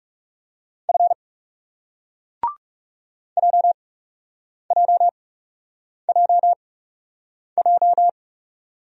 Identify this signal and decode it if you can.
Morse code, heard as a keyed tone, about 41 wpm, spaced as for 8 wpm, 700 Hz. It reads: F JJJJ